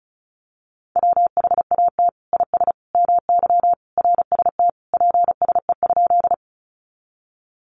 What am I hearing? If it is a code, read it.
Morse code, heard as a keyed tone, about 34 wpm, 700 Hz. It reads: WHAT IS MY RST PSE?
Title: Common QSO Phrases 35wpm